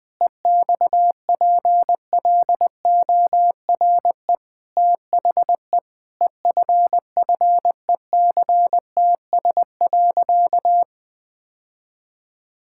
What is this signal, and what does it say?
Morse code, heard as a keyed tone, about 20 wpm, 700 Hz. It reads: EXPLORE THE EFFECTS.
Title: Sentences with 5-Word Phrases - Spoken Beginning - 20wpm